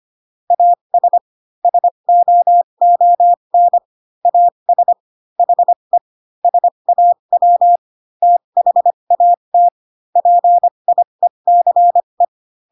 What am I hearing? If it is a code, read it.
Morse code, heard as a keyed tone, about 25 wpm, spaced as for 22 wpm, 700 Hz. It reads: AS SOON AS HE SAW THAT PIECE